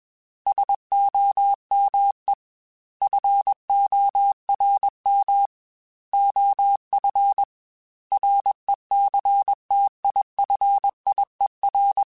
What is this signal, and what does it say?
Morse code, heard as a keyed tone, about 21 wpm, 800 Hz. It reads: SOME FORM OF RECTIFIER
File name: Technical Subjects / Technical Subjects-mp3s / Sw Pwr Sup @20